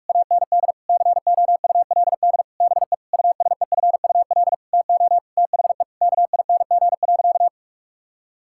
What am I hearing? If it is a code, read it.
Morse code, heard as a keyed tone, about 46 wpm, 700 Hz. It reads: AND COULD BE USEFUL TO THE KING.
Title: Sentences from Top 1000 Words - Review Part 6 - 45wpm